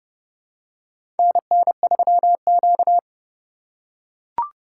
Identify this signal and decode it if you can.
Morse code, heard as a keyed tone, about 30 wpm, 700 Hz. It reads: NN3Q
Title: USA Call Signs - 4 Characters - 3x Character Spacing - 30wpm